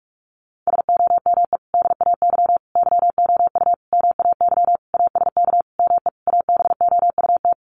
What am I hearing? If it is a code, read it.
Morse code, heard as a keyed tone, about 44 wpm, 700 Hz. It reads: SOME DAY YOU MAY ASK ME ABOUT
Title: Sentences from Top 200 Words 45wpm